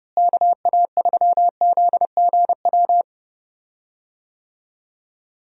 Morse code, heard as a keyed tone, about 31 wpm, 700 Hz. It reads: KA3ZGW
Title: USA Call Signs - 6 Characters - Repeated 3x - 30wpm